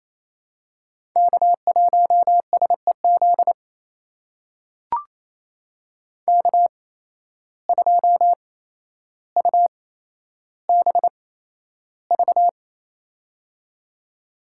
Morse code, heard as a keyed tone, about 28 wpm, spaced as for 12 wpm, 700 Hz. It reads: K1SEZ K2UBV
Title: USA Call Signs - 5 Characters - 8x Character Spacing - 28wpm